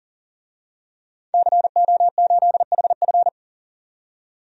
Morse code, heard as a keyed tone, about 40 wpm, 700 Hz. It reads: CO8HF
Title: DXCC-100 Call Signs - 5 Characters - 40wpm